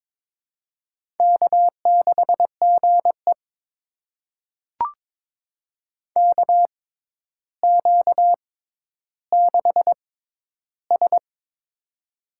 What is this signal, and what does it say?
Morse code, heard as a keyed tone, about 22 wpm, spaced as for 11 wpm, 700 Hz. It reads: K6GE KQ6S
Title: USA Call Signs - 6x Character Spacing - 22wpm